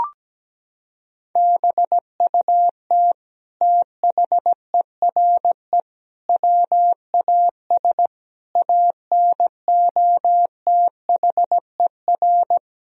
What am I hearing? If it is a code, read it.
Morse code, heard as a keyed tone, about 17 wpm, 700 Hz. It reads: BUT THERE WAS ANOTHER
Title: Sentences from Top 1000 Words - Review Part 2 - 17wpm